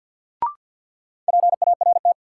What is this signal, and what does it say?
Morse code, heard as a keyed tone, about 53 wpm, 700 Hz. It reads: PART
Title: Sentences from Top 200 Words - New Word Drill - 50wpm